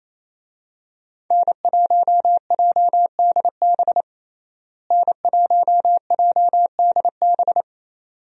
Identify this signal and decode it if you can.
Morse code, heard as a keyed tone, about 28 wpm, 700 Hz. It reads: N1JDB N1JDB
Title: USA Call Signs - 5 Characters - Repeated 2x - 28wpm